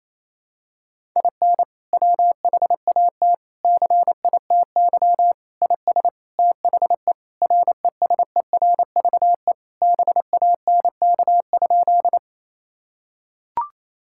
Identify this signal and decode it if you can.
Morse code, heard as a keyed tone, about 27 wpm, 700 Hz. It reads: IN WHAT CITY IS THE RESERVE BANK?
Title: Sentences from Top 1000 Words - Review Part 6 - 28wpm